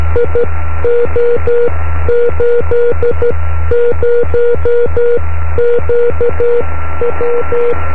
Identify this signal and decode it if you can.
Morse code, heard as a keyed tone, about 12 wpm, spaced as for 11 wpm, 480 Hz. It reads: IO80QW
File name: om3pv 70mhz / zaznam / gb3wsx